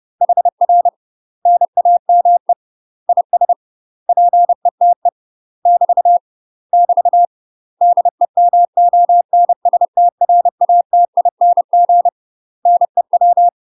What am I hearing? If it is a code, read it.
Morse code, heard as a keyed tone, about 30 wpm, 700 Hz. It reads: HR NAME IS PETE <BT> <BT> DEMONSTRATING NEW
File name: UK MASTER QSOs TRAINING / 30WPM / UK MASTER QSOs 0043